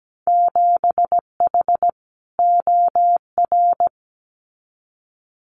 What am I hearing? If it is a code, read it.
Morse code, heard as a keyed tone, about 17 wpm, 700 Hz. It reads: 7H OR
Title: Winter Field Day Exchanges Encore 17wpm